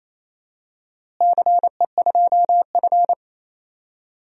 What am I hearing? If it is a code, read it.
Morse code, heard as a keyed tone, about 28 wpm, 700 Hz. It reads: CE2F
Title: DXCC-100 Call Signs - 4 Characters - Repeated 3x - 28wpm